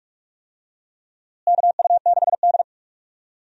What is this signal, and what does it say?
Morse code, heard as a keyed tone, about 41 wpm, 700 Hz. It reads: KU6D
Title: USA Call Signs - 4 Characters - Repeated 3x - 45wpm